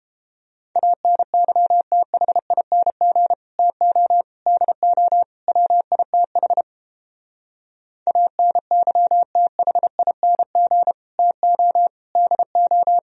Morse code, heard as a keyed tone, about 30 wpm, 700 Hz. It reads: ANYTHING TO DO WITH ANYTHING TO DO